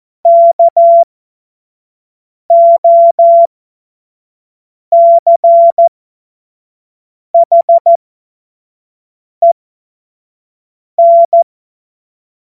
Morse code, heard as a keyed tone, about 14 wpm, spaced as for 5 wpm, 680 Hz. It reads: KOCHEN
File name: Morsekurs / Witze / Witz 06 Februar 2024 25 F70